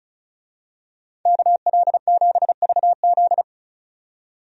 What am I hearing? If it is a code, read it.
Morse code, heard as a keyed tone, about 36 wpm, 700 Hz. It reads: KL7VZ
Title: DXCC-100 Call Signs - 5 Characters - 35wpm